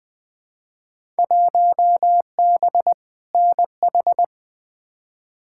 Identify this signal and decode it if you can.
Morse code, heard as a keyed tone, about 20 wpm, 700 Hz. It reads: 1B NH
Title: Field Day Exchanges Encore - 20wpm